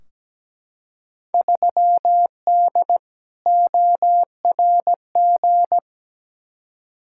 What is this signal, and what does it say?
Morse code, heard as a keyed tone, about 17 wpm, 700 Hz. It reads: 3D ORG
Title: Field Day Exchanges Encore - 17wpm